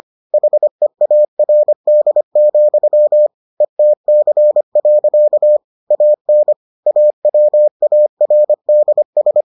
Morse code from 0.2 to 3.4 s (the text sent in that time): HEARD,